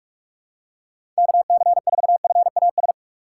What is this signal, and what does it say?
Morse code, heard as a keyed tone, about 47 wpm, 700 Hz. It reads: KC4FAS
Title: USA Call Signs - 6 Characters - 4x Character Spacing - 45wpm